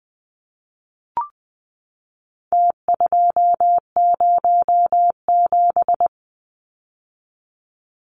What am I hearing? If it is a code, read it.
Morse code, heard as a keyed tone, about 20 wpm, 700 Hz. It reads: T207
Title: Numbers - Contest Style - 4-Digit 20wpm